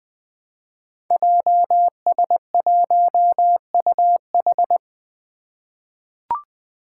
Morse code, heard as a keyed tone, about 20 wpm, 700 Hz. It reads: JS1UH
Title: Five Challenging Letter-Number Combinations 20wpm